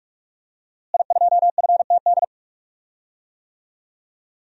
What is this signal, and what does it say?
Morse code, heard as a keyed tone, about 42 wpm, 700 Hz. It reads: I2FTD
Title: DXCC-100 Call Signs - 5 Characters - Repeated 3x - 45wpm